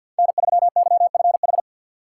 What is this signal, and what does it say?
Morse code, heard as a keyed tone, about 52 wpm, 690 Hz. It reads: N3YFH
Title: USA Call Signs - 5 Characters - 4x Character Spacing - 50wpm